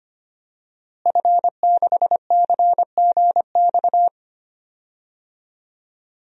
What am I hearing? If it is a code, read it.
Morse code, heard as a keyed tone, about 25 wpm, 700 Hz. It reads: F6CGX
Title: DXCC-100 Call Signs - 5 Characters - Repeated 3x - 25wpm